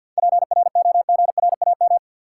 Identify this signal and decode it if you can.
Morse code, heard as a keyed tone, about 53 wpm, 700 Hz. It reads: PROGRAM
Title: Sentences from Top 1000 Words - Review Part 4 - New Word Drill - 50wpm